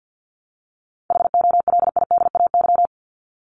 Morse code, heard as a keyed tone, about 50 wpm, 700 Hz. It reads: HOLIDAY
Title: Top 1501-1750 Words 50wpm